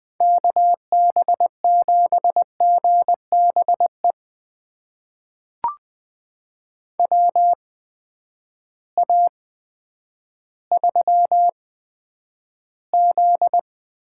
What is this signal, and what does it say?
Morse code, heard as a keyed tone, about 20 wpm, spaced as for 10 wpm, 700 Hz. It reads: KB7GBE WA3Z
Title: USA Call Signs - 6 Characters - 8x Character Spacing - 20wpm